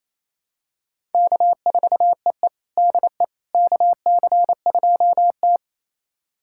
Morse code, heard as a keyed tone, about 27 wpm, 690 Hz. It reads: K4EE DE KC2T
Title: USA Callsign DE Callsign - 4 Characters - 28wpm